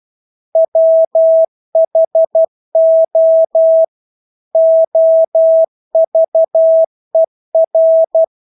Morse code, heard as a keyed tone, about 12 wpm, 650 Hz. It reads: WHO OVER
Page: Engelske ord 12 wpm